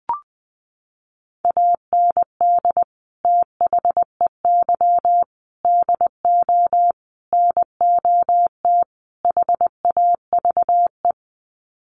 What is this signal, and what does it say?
Morse code, heard as a keyed tone, about 20 wpm, 700 Hz. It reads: AND THEY DO NOT HAVE
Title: Sets of 5 Words Encore2 20wpm